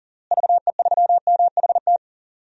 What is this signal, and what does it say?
Morse code, heard as a keyed tone, about 40 wpm, 700 Hz. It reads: VE3MHT
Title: DXCC-100 Call Signs - 6 Characters - Repeated 3x - 40wpm